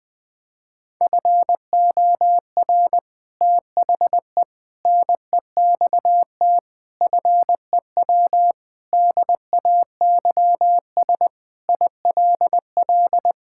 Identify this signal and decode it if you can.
Morse code, heard as a keyed tone, about 20 wpm, 700 Hz. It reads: FOR THE NEXT FEW DAYS ILL
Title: Sentences with 3-Word Phrases - 20wpm